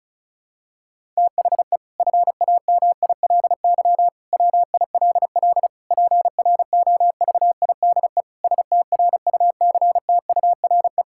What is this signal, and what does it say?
Morse code, heard as a keyed tone, about 34 wpm, 700 Hz. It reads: THE FAMILY WILL PROVIDE STRUCTURE
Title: Sentences from Top 1000 Words - Review Part 6 - 35wpm